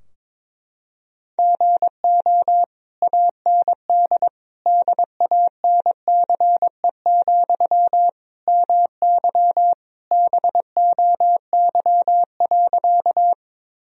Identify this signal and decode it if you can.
Morse code, heard as a keyed tone, about 22 wpm, 700 Hz. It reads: GO AND DANCE, MY BOY.
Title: Sentences from Top 1000 Words - Review Part 4 - 22wpm